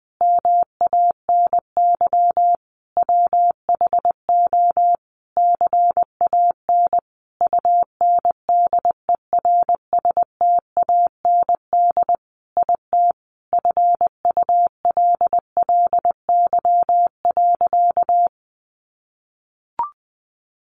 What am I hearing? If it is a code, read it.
Morse code, heard as a keyed tone, about 20 wpm, 700 Hz. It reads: MANY WHO CAN UNDERSTAND IT FULLY.
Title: Sentences from Top 1300 Words 20wpm